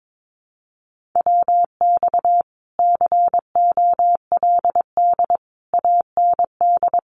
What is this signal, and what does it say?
Morse code, heard as a keyed tone, about 22 wpm, 700 Hz. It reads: WX COLD AND
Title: Common QSO Phrases Encore 22wpm